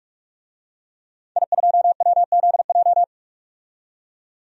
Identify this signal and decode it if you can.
Morse code, heard as a keyed tone, about 45 wpm, 700 Hz. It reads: I2WZJ
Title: DXCC-100 Call Signs - 5 Characters - 45wpm